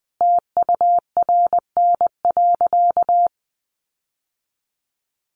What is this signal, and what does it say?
Morse code, heard as a keyed tone, about 20 wpm, 700 Hz. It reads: TURN.
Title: Sentences from Top 1500 Words 20wpm